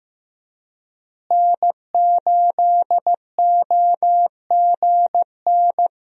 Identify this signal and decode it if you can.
Morse code, heard as a keyed tone, about 15 wpm, 700 Hz. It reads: N8OGN
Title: USA Call Signs - 5 Characters - Repeated 2x - 15wpm